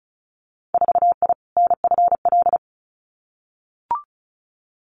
Morse code, heard as a keyed tone, about 36 wpm, 700 Hz. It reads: 4I NFL
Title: Winter Field Day Exchanges 35wpm